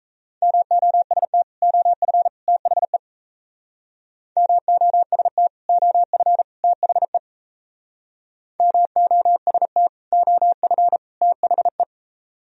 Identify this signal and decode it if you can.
Morse code, heard as a keyed tone, about 35 wpm, 700 Hz. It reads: MOST OF THE MOST OF THE MOST OF THE
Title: Speed-Racing - Sets of 3 Words 28wpm